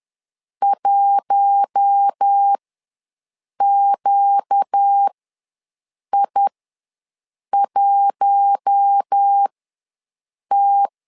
Morse code, heard as a keyed tone, about 10 wpm, spaced as for 6 wpm, 800 Hz. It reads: 1QI1T
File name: LEZIONI / 4 Q1 ws 10wpm 2538